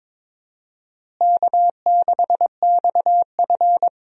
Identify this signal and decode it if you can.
Morse code, heard as a keyed tone, about 22 wpm, 700 Hz. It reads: K6XF
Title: USA Call Signs - 4 Characters - Repeated 2x - 22wpm